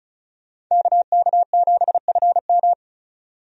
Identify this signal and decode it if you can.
Morse code, heard as a keyed tone, about 34 wpm, 700 Hz. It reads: KK7FM
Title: USA Call Signs - 5 Characters - 35wpm